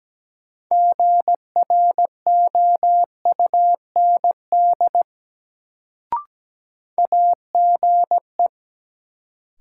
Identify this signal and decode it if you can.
Morse code, heard as a keyed tone, about 17 wpm, 700 Hz. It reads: GROUND AGE
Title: Sentences from Top 500 Words - New Word Drill - 17wpm